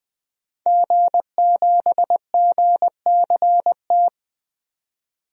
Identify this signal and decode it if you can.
Morse code, heard as a keyed tone, about 20 wpm, 700 Hz. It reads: G7GCT
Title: DXCC-100 Call Signs - 5 Characters - 20wpm